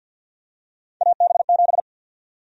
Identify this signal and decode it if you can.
Morse code, heard as a keyed tone, about 52 wpm, 700 Hz. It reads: AB7
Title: USA Call Sign Prefixes - Repeated 2x - 50wpm